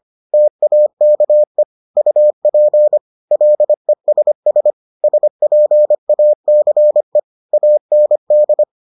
Morse code, heard as a keyed tone, about 25 wpm, 600 Hz. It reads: TAKE UP LESS SPACE AND